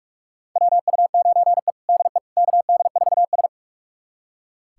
Morse code, heard as a keyed tone, about 46 wpm, 700 Hz. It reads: WU9E DE KD4S
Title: USA Callsign DE Callsign - 4 Characters - 45wpm